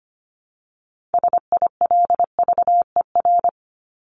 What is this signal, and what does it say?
Morse code, heard as a keyed tone, about 25 wpm, 700 Hz. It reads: SILVER
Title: Top 1751-2000 Words 25wpm